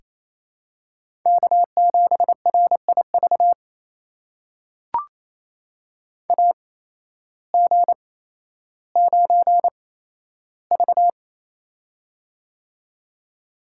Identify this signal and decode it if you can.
Morse code, heard as a keyed tone, about 28 wpm, spaced as for 13 wpm, 700 Hz. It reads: K7RIV AG9V
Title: USA Call Signs - 8x Character Spacing - 28wpm